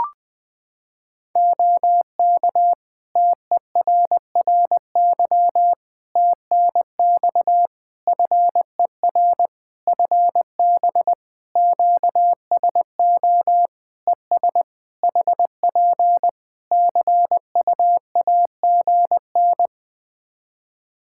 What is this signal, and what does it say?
Morse code, heard as a keyed tone, about 19 wpm, 700 Hz. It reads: OK TERRY TNX FER FB QSO ES HP CUAGN
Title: Practice QSO - Exchange 4 - Ending - 20wpm